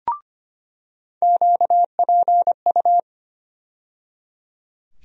Morse code, heard as a keyed tone, about 25 wpm, 700 Hz. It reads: QPU
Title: USA Call Sign Postfixes - 3 Letters 25wpm